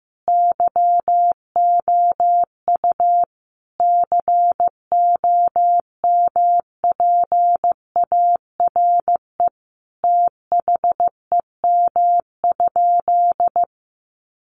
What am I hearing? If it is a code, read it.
Morse code, heard as a keyed tone, about 15 wpm, 700 Hz. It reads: YOU COMPARE THEM?
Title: Sentences from Top 1300 Words 15wpm